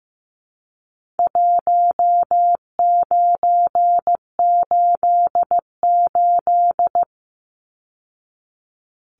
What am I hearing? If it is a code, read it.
Morse code, heard as a keyed tone, about 15 wpm, 700 Hz. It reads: 1988
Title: Numbers - Contest Style - 4-Digit 15wpm